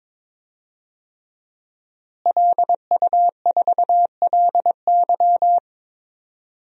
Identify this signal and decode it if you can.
Morse code, heard as a keyed tone, about 22 wpm, 700 Hz. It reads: LU4LY
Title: DXCC-100 Call Signs - 5 Characters - Repeated 3x - 22wpm